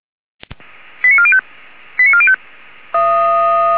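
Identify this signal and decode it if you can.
Five-tone selective call (ZVEI), heard as quick runs of five tones, tones of about 70 ms. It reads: 89405 89405
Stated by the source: ZVEI five-tone selective call received on SDR, code 89405.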